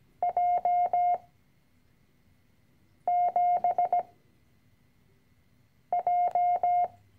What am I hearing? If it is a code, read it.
Morse code, heard as a keyed tone, about 17 wpm, spaced as for 4 wpm, 710 Hz. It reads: J7J
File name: CW-Online-Kurs / Anfänger / Lektion 010